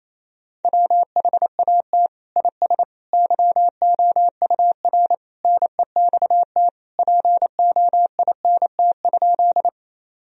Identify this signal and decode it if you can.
Morse code, heard as a keyed tone, about 28 wpm, 700 Hz. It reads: WHAT IS YOUR NEXT POINT?
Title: Sentences from Top 1000 Words - Review Part 1 - 28wpm